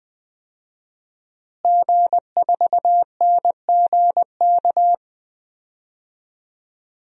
Morse code, heard as a keyed tone, about 20 wpm, 700 Hz. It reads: G4NGK
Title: DXCC-100 Call Signs - 5 Characters - Repeated 3x - 20wpm